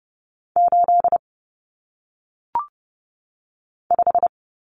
Morse code, heard as a keyed tone, about 30 wpm, 700 Hz. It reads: 8 5
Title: Numbers - 1-Digit 30wpm